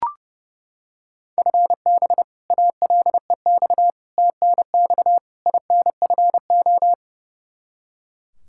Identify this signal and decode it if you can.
Morse code, heard as a keyed tone, about 32 wpm, 700 Hz. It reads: FB ALEX TNX INFO
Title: Practice QSO - Exchange 3 - More Info - 30wpm